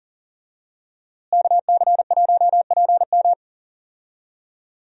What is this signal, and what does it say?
Morse code, heard as a keyed tone, about 37 wpm, 700 Hz. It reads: KC1PM
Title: USA Call Signs - 5 Characters - Repeated 2x - 40wpm